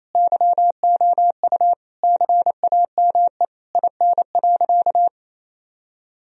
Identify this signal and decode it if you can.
Morse code, heard as a keyed tone, about 28 wpm, 700 Hz. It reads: YOU CAME IN.
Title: Sentences from Top 1000 Words - Review Part 3 - 28wpm